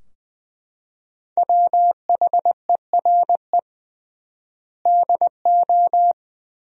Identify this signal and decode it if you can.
Morse code, heard as a keyed tone, about 20 wpm, 700 Hz. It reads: WHERE DO
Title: 3x Word-Spacing - Sentences from Top 200 Words - 20wpm